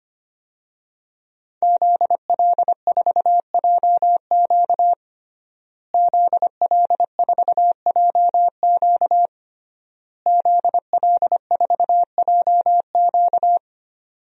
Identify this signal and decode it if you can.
Morse code, heard as a keyed tone, about 25 wpm, 700 Hz. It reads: ZL4JQ ZL4JQ ZL4JQ
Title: DXCC-100 Call Signs - 5 Characters - Repeated 3x - 25wpm